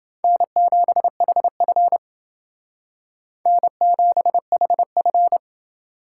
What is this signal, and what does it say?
Morse code, heard as a keyed tone, about 28 wpm, 700 Hz. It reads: N7HF N7HF